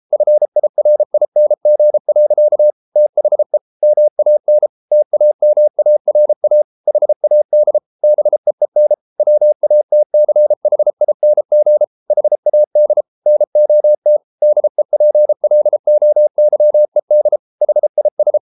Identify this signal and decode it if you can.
Morse code, heard as a keyed tone, about 33 wpm, 600 Hz. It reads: FIRING. THE MAN TAMARA HAD BEEN WATCHING HAD NOT DEPLOYED HIS